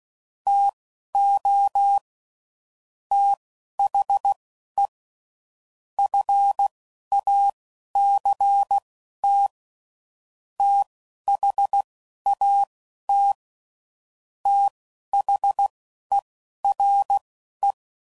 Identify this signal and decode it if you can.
Morse code, heard as a keyed tone, about 16 wpm, spaced as for 11 wpm, 800 Hz. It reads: TO THE FACT THAT THERE
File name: CW Code Groups @10WPM / Technical Subjects / AND Gate